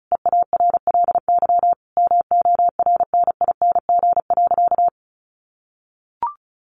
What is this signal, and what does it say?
Morse code, heard as a keyed tone, about 34 wpm, 700 Hz. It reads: EARLY MORNING.
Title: Sentences from Top 1300 Words 35wpm